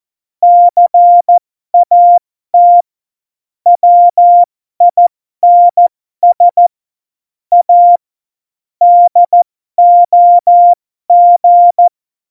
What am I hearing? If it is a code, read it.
Morse code, heard as a keyed tone, about 14 wpm, spaced as for 12 wpm, 700 Hz. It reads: CAT WINS A DOG
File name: Morsekurs / Witze